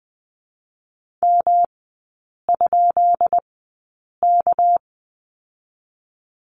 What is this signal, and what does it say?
Morse code, heard as a keyed tone, about 20 wpm, 700 Hz. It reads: M ? K
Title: 198 Lesson 32e - Letter k - Three Characters 20wpm